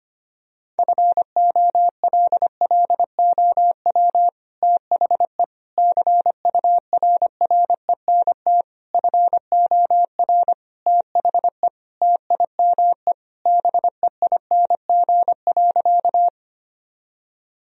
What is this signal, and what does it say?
Morse code, heard as a keyed tone, about 25 wpm, 700 Hz. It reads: FOLLOW THE CURRENT FOR THE TIME BEING.
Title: Sentences from Top 1000 Words - Review Part 4 - 25wpm